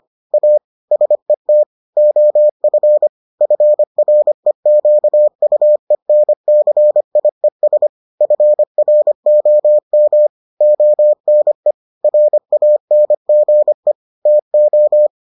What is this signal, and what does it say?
Morse code, heard as a keyed tone, about 25 wpm, 600 Hz. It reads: A SET OF FREQUENCIES FROM ONE RANGE TO